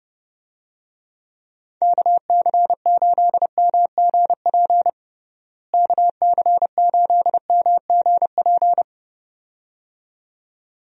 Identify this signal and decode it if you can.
Morse code, heard as a keyed tone, about 32 wpm, 700 Hz. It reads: KC8MGP KC8MGP
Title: USA Call Signs - 6 Characters - Repeated 2x - 30wpm